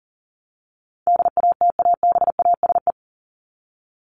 Morse code, heard as a keyed tone, about 40 wpm, 700 Hz. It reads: DATABASE
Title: Top 1751-2000 Words 40wpm